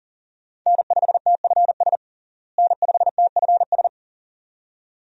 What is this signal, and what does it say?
Morse code, heard as a keyed tone, about 40 wpm, 700 Hz. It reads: N5TFS N5TFS
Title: USA Call Signs - 5 Characters - Repeated 2x - 40wpm